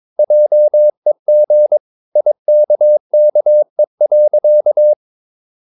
Morse code, heard as a keyed tone, about 22 wpm, 600 Hz. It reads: JEG IKKE.
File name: Ferie QRS 22wpm